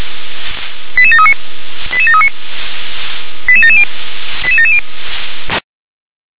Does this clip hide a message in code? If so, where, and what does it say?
Yes: five-tone selective call (ZVEI) from 1.0 to 4.8 s, heard as quick runs of five tones, tones of about 70 ms. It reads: 88739 88739 88770 88770